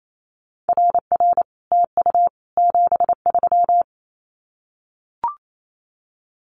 Morse code, heard as a keyed tone, about 28 wpm, 700 Hz. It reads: RR TU 73
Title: Common QSO Phrases Encore 28wpm